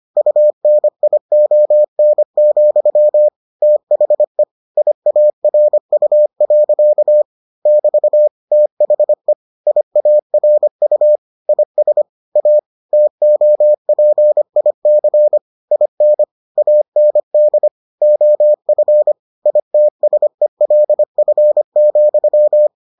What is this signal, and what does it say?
Morse code, heard as a keyed tone, about 25 wpm, 600 Hz. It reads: UNION, THE IARU. <BT> THE IARU IS A TOPIC IN AND OF ITSELF,